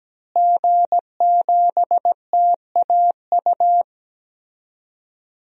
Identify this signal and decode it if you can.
Morse code, heard as a keyed tone, about 17 wpm, 700 Hz. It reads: G7TAU
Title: DXCC-100 Call Signs - 5 Characters - Repeated 2x - 17wpm